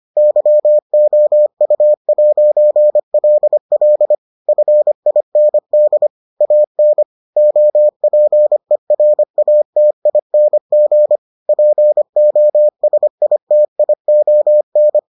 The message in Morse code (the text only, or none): YOU'LL FIND AN OPERATING POSITION